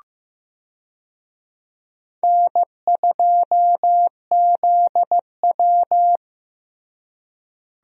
Morse code, heard as a keyed tone, about 15 wpm, 700 Hz. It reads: N2ZW
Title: USA Call Signs - 4 Characters - Repeated 2x - 15wpm